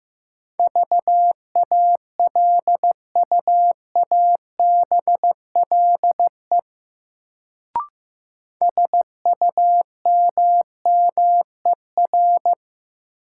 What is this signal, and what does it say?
Morse code, heard as a keyed tone, about 15 wpm, 700 Hz. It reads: VALUABLE SUMMER
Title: Sentences from Top 1100 Words - New Word Drill Alt - 15wpm